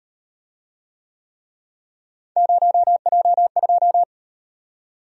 Morse code, heard as a keyed tone, about 30 wpm, 700 Hz. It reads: 0J2
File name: Speed-Racing - Three Letter-Number Combinations - 25wpm